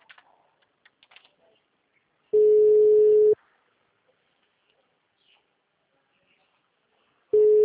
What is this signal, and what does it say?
Ringback tone (Europe, 425 Hz).